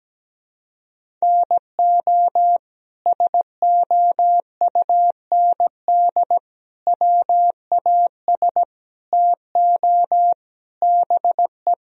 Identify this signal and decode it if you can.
Morse code, heard as a keyed tone, about 17 wpm, 700 Hz. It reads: NO SOUND WAS TO BE
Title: Sentences from Top 1000 Words - Review Part 1 - 17wpm